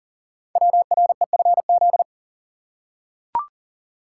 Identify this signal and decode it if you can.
Morse code, heard as a keyed tone, about 42 wpm, 700 Hz. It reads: WREFZ
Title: Code Groups - 5 Letter:Numbers - 40wpm